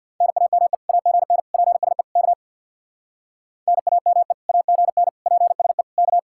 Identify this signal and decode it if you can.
Morse code, heard as a keyed tone, about 64 wpm, 700 Hz. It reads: NAME AGN PSE K NAME AGN PSE K